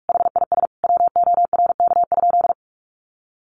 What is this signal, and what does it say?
Morse code, heard as a keyed tone, about 44 wpm, 700 Hz. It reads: HIS WORK?
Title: Sentences from Top 1300 Words 45wpm